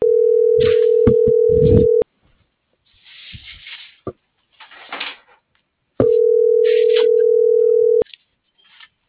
Ringback tone (North America), 440 + 480 Hz.